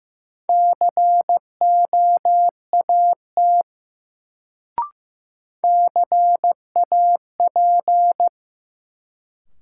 Morse code, heard as a keyed tone, about 15 wpm, 700 Hz. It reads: COAT CAP
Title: Sentences from Top 1600 Words - New Word Drill Alt - 15wpm